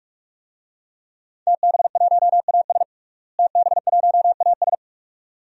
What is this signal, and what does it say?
Morse code, heard as a keyed tone, about 42 wpm, 700 Hz. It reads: TB1AS TB1AS
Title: DXCC-100 Call Signs - 5 Characters - Repeated 2x - 45wpm